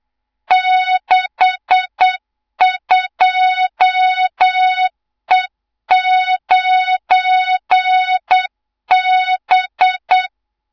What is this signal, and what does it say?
Morse code, heard as a keyed tone, about 8 wpm, 3700 Hz, with noise. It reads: 62E9B
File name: Morze-HA2MN / hang / 24 GYAKORLAT